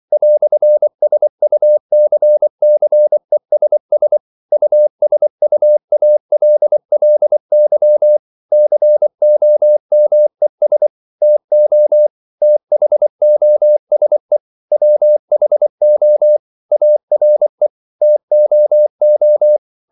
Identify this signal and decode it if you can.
Morse code, heard as a keyed tone, about 24 wpm, 600 Hz. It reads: "SUCCESS USUALLY COMES TO THOSE WHO ARE TOO
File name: CW / Practice / EnglishQuotations0224